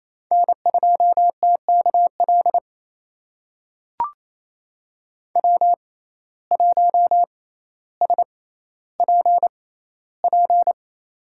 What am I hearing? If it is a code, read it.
Morse code, heard as a keyed tone, about 28 wpm, spaced as for 14 wpm, 700 Hz. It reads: N2TKL W1SPP
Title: USA Call Signs - 5 Characters - 6x Character Spacing - 28wpm